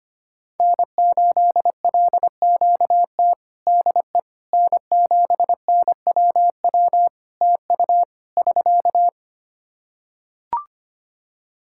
Morse code, heard as a keyed tone, about 25 wpm, 700 Hz. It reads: N8LQT DE N7NWW TU <SK>
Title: Practice QSO - Exchange 4 - Ending - 25wpm